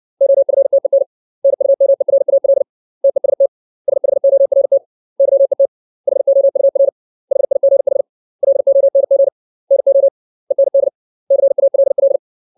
Morse code, heard as a keyed tone, about 60 wpm, 550 Hz. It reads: OFTEN NUMERAL TEST SHORT YET HOUR HIGH LONG NO END CALL
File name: 632 words / 632 Words 60WPM